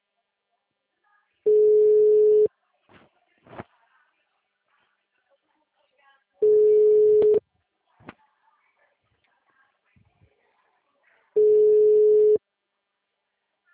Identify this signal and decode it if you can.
Ringback tone (Europe, 425 Hz).
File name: out- -2013- - - .1279